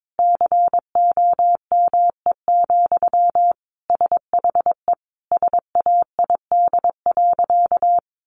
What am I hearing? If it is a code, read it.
Morse code, heard as a keyed tone, about 22 wpm, 700 Hz. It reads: COME, SHE SAID.
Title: Sentences from Top 100 Words 22wpm